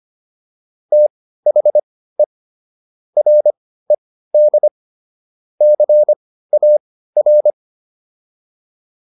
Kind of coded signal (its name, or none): Morse code